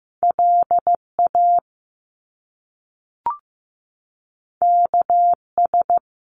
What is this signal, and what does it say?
Morse code, heard as a keyed tone, about 15 wpm, 700 Hz. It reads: LA KS
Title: US States 15wpm